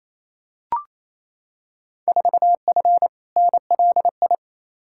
Morse code, heard as a keyed tone, about 28 wpm, 700 Hz. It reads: 4F NLI
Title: Field Day Exchanges Encore - 28wpm